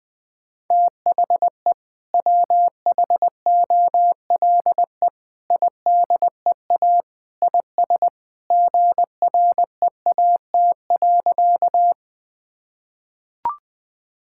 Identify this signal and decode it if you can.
Morse code, heard as a keyed tone, about 20 wpm, 700 Hz. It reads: THE WHOLE IDEA IS GREAT.
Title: Sentences with 2-Word Phrases - 20wpm